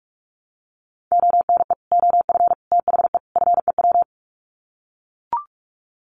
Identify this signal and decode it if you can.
Morse code, heard as a keyed tone, about 44 wpm, 700 Hz. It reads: ONE OF THE FEW
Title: Sets of 4 Words 45wpm